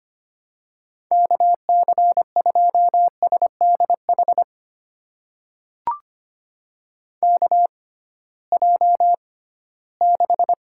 Morse code, heard as a keyed tone, about 25 wpm, spaced as for 16 wpm, 700 Hz. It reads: KC2SDH KJ6
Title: USA Call Signs - 6 Characters - 6x Character Spacing - 25wpm